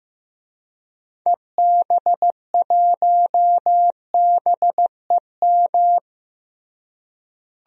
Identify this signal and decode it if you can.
Morse code, heard as a keyed tone, about 15 wpm, 700 Hz. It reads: EB1BEM
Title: DXCC-100 Call Signs - 6 Characters - Repeated 3x - 15wpm